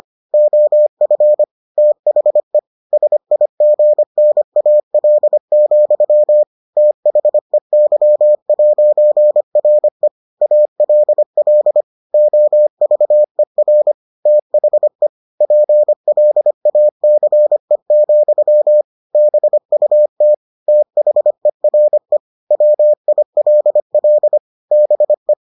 Morse code, heard as a keyed tone, about 25 wpm, 600 Hz. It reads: OF THE SIGNAL, THEY'RE ALL OVER THE PLACE, BUT THERE WILL BE